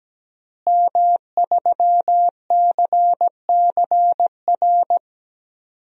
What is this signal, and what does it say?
Morse code, heard as a keyed tone, about 17 wpm, 700 Hz. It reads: M3CCR
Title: DXCC-100 Call Signs - 5 Characters - Repeated 2x - 17wpm